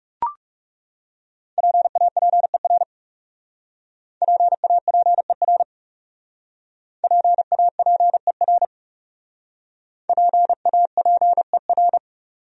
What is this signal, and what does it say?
Morse code, heard as a keyed tone, about 38 wpm, 700 Hz. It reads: PAPER PAPER PAPER PAPER
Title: Speed-Racing Top 500 Words 30wpm